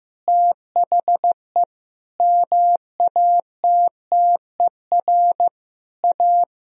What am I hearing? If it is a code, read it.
Morse code, heard as a keyed tone, about 15 wpm, 700 Hz. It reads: THE MATTER A
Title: Sentences from Top 1000 Words - Review Part 2 - 15wpm